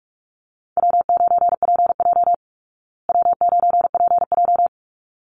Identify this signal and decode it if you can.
Morse code, heard as a keyed tone, about 44 wpm, 700 Hz. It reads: W9PJ W9PJ
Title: Call Signs Repeated 3x - 45wpm